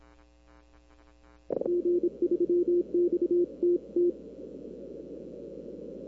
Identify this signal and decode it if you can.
Morse code, heard as a keyed tone, about 26 wpm, 350 Hz. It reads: G3XTT
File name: arrl cw dx